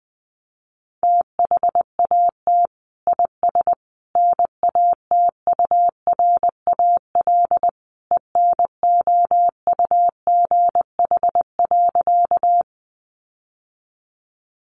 Morse code, heard as a keyed tone, about 20 wpm, 700 Hz. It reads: THAT IS NATURAL ENOUGH.
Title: Sentences from Top 700 Words 20wpm